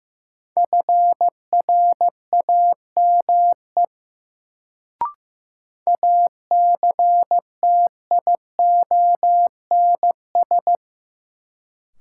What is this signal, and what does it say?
Morse code, heard as a keyed tone, about 15 wpm, 700 Hz. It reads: FRAME ACTIONS
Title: Sentences from Top 1000 Words - Review Part 6 - New Word Drill - 15wpm